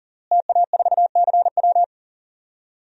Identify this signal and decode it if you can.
Morse code, heard as a keyed tone, about 40 wpm, 700 Hz. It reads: TA4CW
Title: DXCC-100 Call Signs - 5 Characters - Repeated 2x - 40wpm